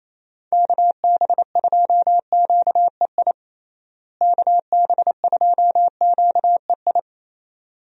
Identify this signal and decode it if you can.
Morse code, heard as a keyed tone, about 28 wpm, 700 Hz. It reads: KB2QEI KB2QEI
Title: USA Call Signs - 6 Characters - Repeated 2x - 28wpm